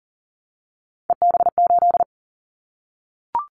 Morse code, heard as a keyed tone, about 42 wpm, 700 Hz. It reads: EB8
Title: Callsign Prefixes for Top 50 DXCC - 40wpm